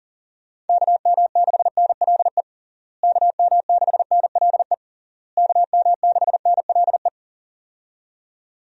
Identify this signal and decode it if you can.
Morse code, heard as a keyed tone, about 40 wpm, 700 Hz. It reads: KM6NLE KM6NLE KM6NLE
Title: USA Call Signs - 6 Characters - Repeated 3x - 40wpm